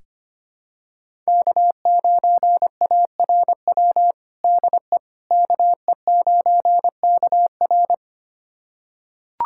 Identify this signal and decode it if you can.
Morse code, heard as a keyed tone, about 25 wpm, 700 Hz. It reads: K9ARW DE KE9KR
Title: USA Callsign DE Callsign - 5 Characters - 25wpm